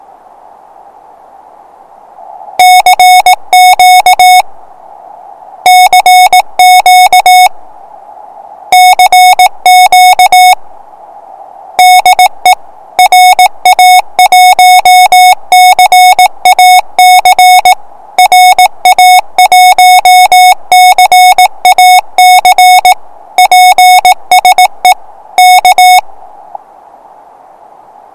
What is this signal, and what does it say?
Morse code, heard as a keyed tone, about 18 wpm, 730 Hz. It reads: CQ CQ CQ DE RA1CAC RA1CAC PSE K
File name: attachments / 1cq noagc 2cq agcin 3cq agcout